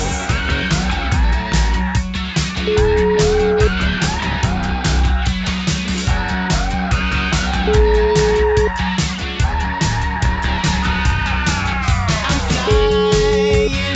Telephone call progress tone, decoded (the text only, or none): ringback tone